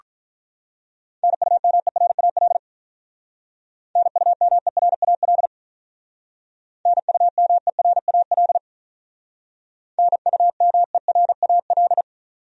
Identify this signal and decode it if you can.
Morse code, heard as a keyed tone, about 44 wpm, 700 Hz. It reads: NUMERAL NUMERAL NUMERAL NUMERAL